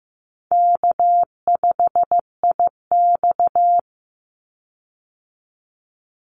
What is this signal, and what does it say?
Morse code, heard as a keyed tone, about 15 wpm, 700 Hz. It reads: K5IX
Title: Call Signs Repeated 3x - 15wpm